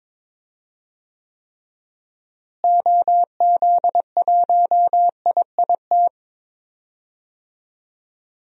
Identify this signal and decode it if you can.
Morse code, heard as a keyed tone, about 21 wpm, 700 Hz. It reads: OZ1IIT
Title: DXCC-100 Call Signs - 6 Characters - Repeated 3x - 22wpm